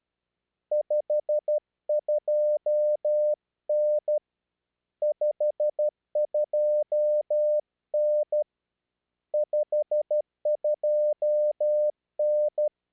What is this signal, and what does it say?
Morse code, heard as a keyed tone, about 12 wpm, 600 Hz. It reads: 52N 52N 52N
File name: /eme/rw3bp